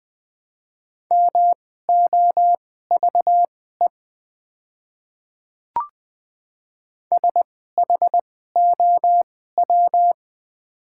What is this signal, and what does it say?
Morse code, heard as a keyed tone, about 20 wpm, spaced as for 14 wpm, 700 Hz. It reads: MOVE SHOW
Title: 2x Character Spacing - Top 200 Words - 20wpm